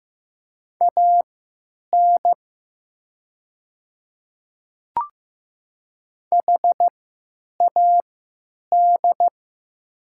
Morse code, heard as a keyed tone, about 15 wpm, spaced as for 8 wpm, 700 Hz. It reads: AN HAD
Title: 3x Character Spacing - Top 100 Words - 15wpm